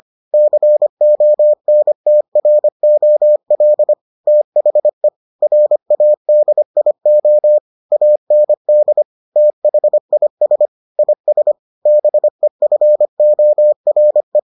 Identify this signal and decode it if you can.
Morse code, heard as a keyed tone, about 25 wpm, 600 Hz. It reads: CONTROL THE RADIO AND THIS IS BEFORE